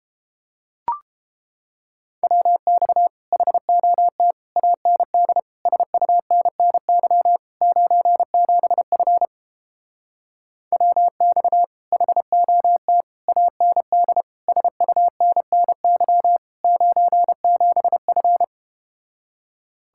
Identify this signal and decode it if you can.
Morse code, heard as a keyed tone, about 31 wpm, 700 Hz. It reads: WX HOT AND SUNNY 97F WX HOT AND SUNNY 97F